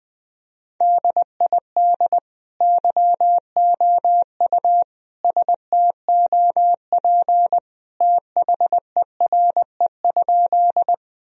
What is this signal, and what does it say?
Morse code, heard as a keyed tone, about 20 wpm, 700 Hz. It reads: DID YOU STOP THERE?
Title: Sentences from Top 1000 Words - Review Part 1 - 20wpm